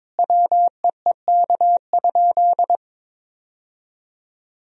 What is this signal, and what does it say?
Morse code, heard as a keyed tone, about 22 wpm, 700 Hz. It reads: WEEK?
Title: Sentences from Top 1000 Words - Review Part 3 - 22wpm